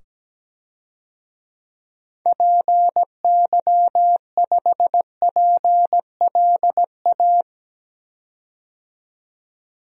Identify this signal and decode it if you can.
Morse code, heard as a keyed tone, about 17 wpm, 700 Hz. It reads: PY5PLA
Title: DXCC-100 Call Signs - 6 Characters - 17wpm